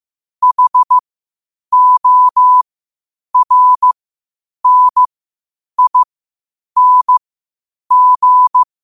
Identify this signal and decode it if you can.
Morse code, heard as a keyed tone, about 15 wpm, spaced as for 8 wpm, 1000 Hz. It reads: HORNING
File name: qotdfiles / 8wpm